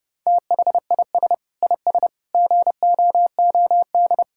Morse code, heard as a keyed tone, about 32 wpm, 700 Hz. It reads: THIS IS GOOD
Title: Sentences from Top 1000 Words - Review Part 6 - 30wpm